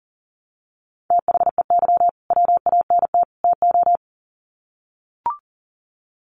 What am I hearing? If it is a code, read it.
Morse code, heard as a keyed tone, about 40 wpm, 700 Hz. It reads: THEY WANT TO
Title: Sets of 3 Words 40wpm